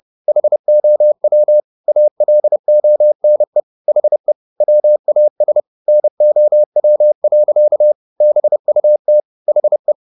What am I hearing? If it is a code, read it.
Morse code, heard as a keyed tone, about 30 wpm, 600 Hz. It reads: HOW ALONE HE WAS NOW. BUT HE
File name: the old man and the sea / the old man and the sea 1 30wpm